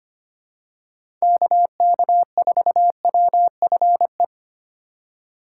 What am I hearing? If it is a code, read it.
Morse code, heard as a keyed tone, about 25 wpm, 700 Hz. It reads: KK4WFE
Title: USA Call Signs - 3x Character Spacing - 25wpm